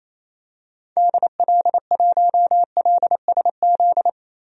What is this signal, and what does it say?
Morse code, heard as a keyed tone, about 28 wpm, 700 Hz. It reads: DL1LSZ
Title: DXCC-100 Call Signs - 6 Characters - 28wpm